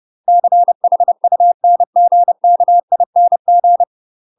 Morse code, heard as a keyed tone, about 30 wpm, 700 Hz. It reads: CHUNGKING
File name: UK MASTER QSOs TRAINING / 30WPM / UK MASTER QSOs 0023